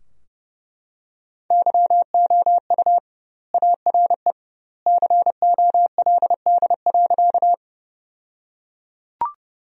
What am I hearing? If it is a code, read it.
Morse code, heard as a keyed tone, about 30 wpm, 700 Hz. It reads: YOU ARE COLD.
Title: 2x Word-Spacing - Sentences from Top 400 Words - 30wpm